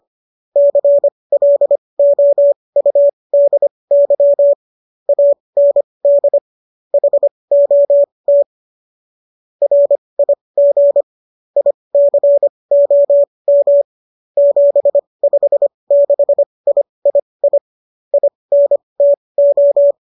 Morse code, heard as a keyed tone, about 25 wpm, spaced as for 20 wpm, 570 Hz. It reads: CLOUDY AND HOT RIG ICOM 756III INTO
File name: qso223 20